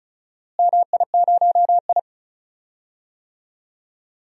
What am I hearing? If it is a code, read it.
Morse code, heard as a keyed tone, about 36 wpm, 700 Hz. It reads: MI0I
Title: DXCC-100 Call Signs - 4 Characters - Repeated 2x - 35wpm